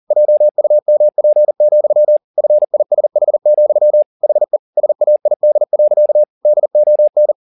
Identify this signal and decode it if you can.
Morse code, heard as a keyed tone, about 40 wpm, 600 Hz. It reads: JUMP, FISH, HE SAID. DON
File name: the old man and the sea 2 40wpm